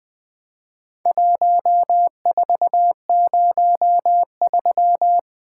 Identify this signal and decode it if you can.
Morse code, heard as a keyed tone, about 20 wpm, 700 Hz. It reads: 1403
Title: Four Letter-Number Combinations - 20wpm